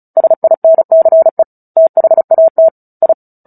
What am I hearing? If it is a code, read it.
Morse code, heard as a keyed tone, about 34 wpm, 650 Hz. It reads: SINCE THAT I
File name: Book Two / 09-Chapter 9 - 35 WPM